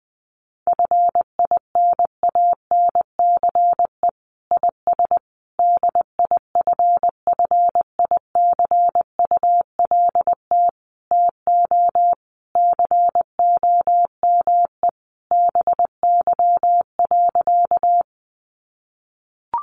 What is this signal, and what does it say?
Morse code, heard as a keyed tone, about 20 wpm, 700 Hz. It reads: FINANCE IS DIFFICULT TO COME BY.
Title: Sentences from Top 1500 Words 20wpm